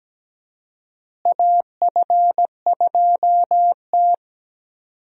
Morse code, heard as a keyed tone, about 17 wpm, 700 Hz. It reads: AF2T
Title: USA Call Signs - 4 Characters - Repeated 3x - 17wpm